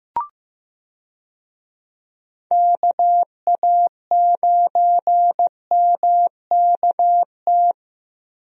Morse code, heard as a keyed tone, about 15 wpm, 700 Hz. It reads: KA9MKT
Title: USA Call Signs - 6 Characters - Repeated 2x - 15wpm